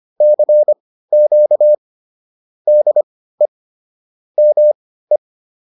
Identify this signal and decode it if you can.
Morse code, heard as a keyed tone, about 25 wpm, spaced as for 15 wpm, 600 Hz. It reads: CQ DE ME